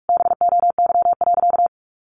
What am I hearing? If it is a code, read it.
Morse code, heard as a keyed tone, about 46 wpm, 700 Hz. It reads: BOY.
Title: Sentences from Top 200 Words 45wpm